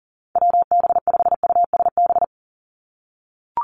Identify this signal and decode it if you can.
Morse code, heard as a keyed tone, about 40 wpm, 700 Hz. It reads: WB5USB
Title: NA SOTA Activator Callsigns 40wpm